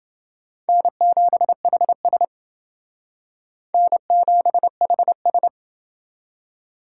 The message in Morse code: N7HS N7HS

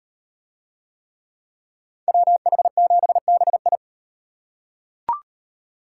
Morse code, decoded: WH7BI

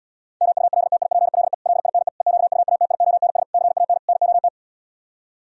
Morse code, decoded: MULTIPLE LINE EQUATIONS CAN NOT

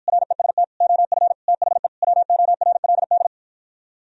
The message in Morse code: REST OF THE WORLD

